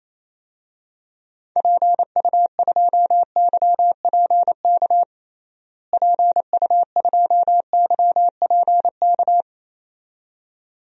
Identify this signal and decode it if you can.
Morse code, heard as a keyed tone, about 27 wpm, 700 Hz. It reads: PU2YPK PU2YPK
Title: DXCC-100 Call Signs - 6 Characters - Repeated 2x - 28wpm